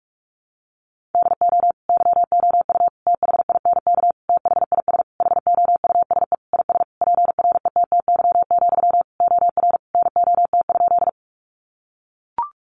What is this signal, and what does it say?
Morse code, heard as a keyed tone, about 44 wpm, 700 Hz. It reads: DO YOU THINK THIS HOUSE IS PRETTY, OR NOT?
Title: Sentences from Top 1000 Words 45wpm